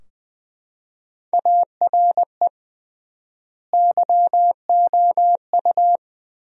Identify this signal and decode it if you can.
Morse code, heard as a keyed tone, about 20 wpm, 700 Hz. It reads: ARE YOU
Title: 3x Word-Spacing - Sentences from Top 200 Words - 20wpm